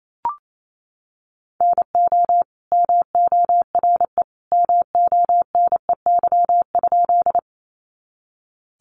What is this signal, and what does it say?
Morse code, heard as a keyed tone, about 28 wpm, 700 Hz. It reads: NO MORE MONEY?
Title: Sentences from Top 400 Words 28wpm